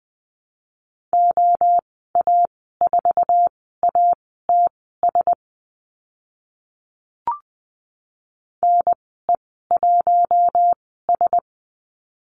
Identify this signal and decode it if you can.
Morse code, heard as a keyed tone, about 20 wpm, spaced as for 15 wpm, 700 Hz. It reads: OA4ATS NE1S
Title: 055 Lesson 10g - Letter r - Call Signs 20wpm